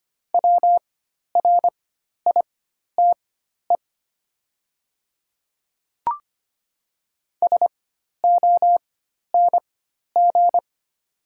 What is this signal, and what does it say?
Morse code, heard as a keyed tone, about 25 wpm, spaced as for 12 wpm, 700 Hz. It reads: WRITE SONG
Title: 4x Character Spacing - Top 500 Words - 25wpm